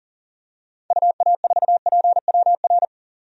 Morse code, heard as a keyed tone, about 40 wpm, 700 Hz. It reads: UA4PWR
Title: DXCC-100 Call Signs - 6 Characters - 40wpm